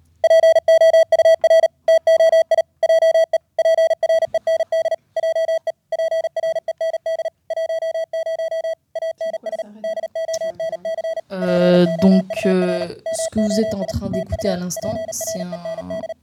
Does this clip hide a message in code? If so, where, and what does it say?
Yes: Morse code from 0.2 to 16.1 s, heard as a keyed tone, about 40 wpm, 660 Hz, with speech. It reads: POUR TOI JE PREND JE PREND 10 ANS DONC POUR MOI TU PREND